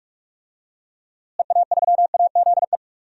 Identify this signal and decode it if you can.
Morse code, heard as a keyed tone, about 48 wpm, 700 Hz. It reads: EA3AZE
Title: DXCC-100 Call Signs - 6 Characters - Repeated 3x - 45wpm